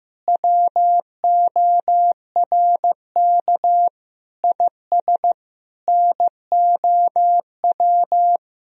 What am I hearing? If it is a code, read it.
Morse code, heard as a keyed tone, about 15 wpm, 700 Hz. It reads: WORK IS NOW
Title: Sentences from Top 400 Words and Abbreviations 15wpm